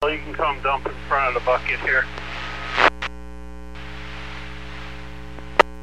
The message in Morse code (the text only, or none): none